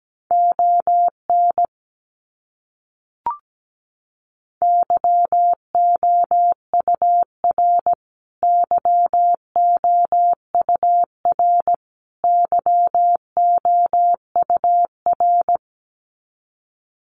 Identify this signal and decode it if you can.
Morse code, heard as a keyed tone, about 17 wpm, 700 Hz. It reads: ON YOUR YOUR YOUR
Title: Top 200 Words Repeated 17wpm